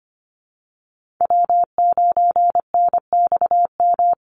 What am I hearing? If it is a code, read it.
Morse code, heard as a keyed tone, about 25 wpm, 700 Hz. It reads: W9NXM
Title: Call Signs Repeated 3x - 25wpm